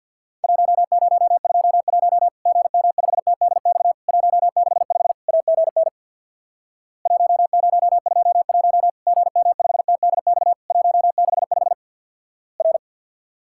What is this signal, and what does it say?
Morse code, heard as a keyed tone, about 52 wpm, 700 Hz. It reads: 1021 GM5TDX 165 AGN 1021 GM5TDX 165 R
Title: Worked All Europe DX Contest - QTC Exchange with Repeats - 50wpm